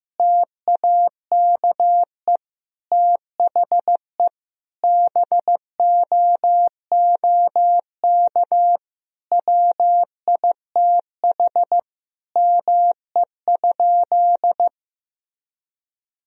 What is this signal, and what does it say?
Morse code, heard as a keyed tone, about 15 wpm, 700 Hz. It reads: TAKE THE BOOK WITH ME?
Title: Sentences from Top 1000 Words - Review Part 1 - 15wpm